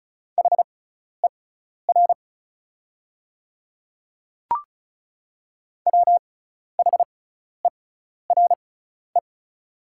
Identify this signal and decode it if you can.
Morse code, heard as a keyed tone, about 36 wpm, spaced as for 12 wpm, 690 Hz. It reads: HER WHERE
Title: 6x Character Spacing - Top 300 Words - 35wpm